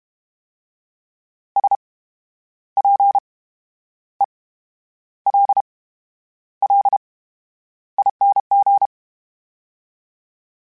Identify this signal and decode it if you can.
Morse code, heard as a keyed tone, about 32 wpm, spaced as for 10 wpm, 800 Hz. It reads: SPELLING
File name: Special INGs / Special INGs-mp3s / VerySlowToFaster Ing @30